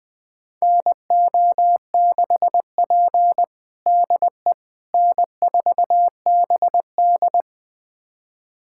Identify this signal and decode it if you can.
Morse code, heard as a keyed tone, about 21 wpm, 700 Hz. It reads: NO6P DE N4BD
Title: USA Callsign DE Callsign - 4 Characters - 20wpm